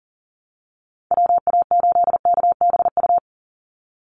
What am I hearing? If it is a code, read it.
Morse code, heard as a keyed tone, about 40 wpm, 700 Hz. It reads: WA8KBU
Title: Call Signs Repeated 2x - 40wpm